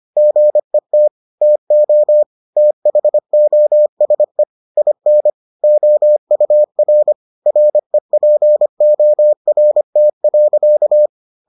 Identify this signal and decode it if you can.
Morse code, heard as a keyed tone, about 25 wpm, 600 Hz. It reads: GET TO THOSE IN OUR REPORT.